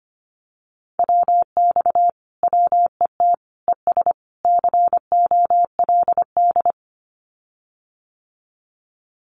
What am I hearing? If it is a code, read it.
Morse code, heard as a keyed tone, about 25 wpm, 700 Hz. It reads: WX WET ES COLD
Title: Common QSO Phrases Encore 25wpm